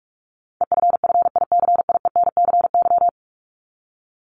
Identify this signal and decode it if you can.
Morse code, heard as a keyed tone, about 44 wpm, 700 Hz. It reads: EFFICIENCY
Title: Top 1501-1750 Words 45wpm